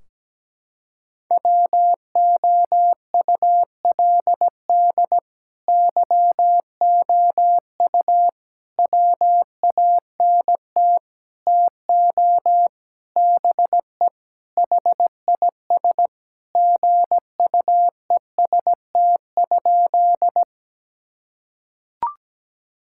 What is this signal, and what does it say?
Morse code, heard as a keyed tone, about 17 wpm, 700 Hz. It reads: WOULD YOU WANT TO BE HIS GUEST?
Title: Sentences from Top 1600 Words 17wpm